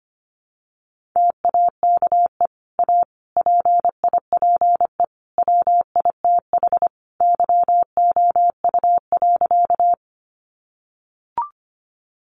Morse code, heard as a keyed tone, about 25 wpm, 700 Hz. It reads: TAKE A PIPE WITH YOU.
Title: Sentences from Top 1500 Words 25wpm